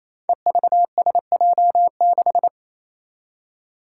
Morse code, heard as a keyed tone, about 27 wpm, 700 Hz. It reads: EVSJ6
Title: Five Challenging Letter-Number Combinations 28wpm